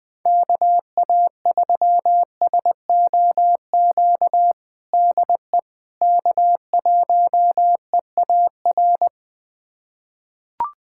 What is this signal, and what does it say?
Morse code, heard as a keyed tone, about 21 wpm, 690 Hz. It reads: KA3SOQ DE K1EAR
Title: USA Callsign DE Callsign 20wpm